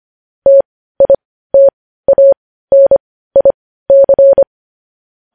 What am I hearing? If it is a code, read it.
Morse code, heard as a keyed tone, about 25 wpm, spaced as for 15 wpm, 550 Hz. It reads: TITANIC
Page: news-headlines-in-morse-code-at-15-wpm